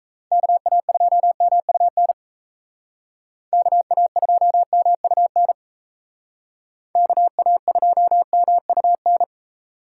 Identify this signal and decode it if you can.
Morse code, heard as a keyed tone, about 35 wpm, 700 Hz. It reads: KA2MUN KA2MUN KA2MUN